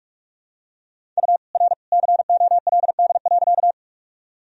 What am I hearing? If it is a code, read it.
Morse code, heard as a keyed tone, about 46 wpm, 700 Hz. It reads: U R COLD.
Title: Sentences from Top 400 Words and Abbreviations 45wpm